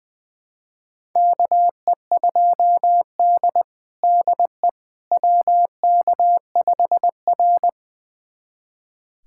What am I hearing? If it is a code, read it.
Morse code, heard as a keyed tone, about 21 wpm, 700 Hz. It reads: KE2D DE WK5R
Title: USA Callsign DE Callsign - 4 Characters - 20wpm